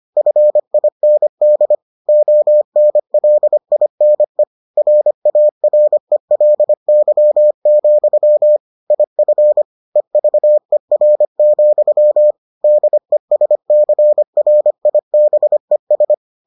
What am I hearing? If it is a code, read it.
Morse code, heard as a keyed tone, about 25 wpm, 600 Hz. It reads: FIND ONLINE RARELY, IF EVER, DESCRIBES